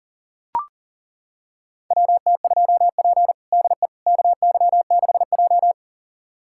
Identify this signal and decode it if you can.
Morse code, heard as a keyed tone, about 40 wpm, 700 Hz. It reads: WT2P DE KY6J
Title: USA Callsign DE Callsign - 4 Characters - 40wpm